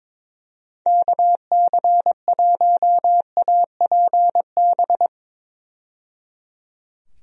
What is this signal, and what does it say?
Morse code, heard as a keyed tone, about 22 wpm, 700 Hz. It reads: KC1APB
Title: USA Call Signs - 6 Characters - 22wpm